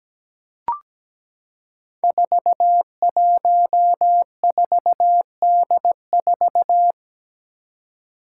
Morse code, heard as a keyed tone, about 17 wpm, 700 Hz. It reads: 414D4
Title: Five Challenging Letter-Number Combinations 17wpm